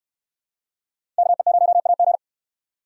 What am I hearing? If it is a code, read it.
Morse code, heard as a keyed tone, about 65 wpm, 700 Hz. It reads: KE0AG